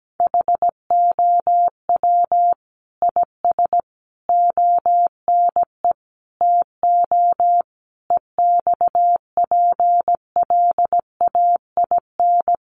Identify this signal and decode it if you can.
Morse code, heard as a keyed tone, about 17 wpm, 700 Hz. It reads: HOW IS ONE TO EXPLAIN
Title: Sentences from Top 1100 Words 17wpm